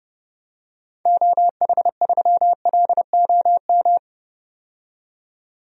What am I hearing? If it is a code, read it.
Morse code, heard as a keyed tone, about 30 wpm, 700 Hz. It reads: OH3LOM
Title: DXCC-100 Call Signs - 6 Characters - 30wpm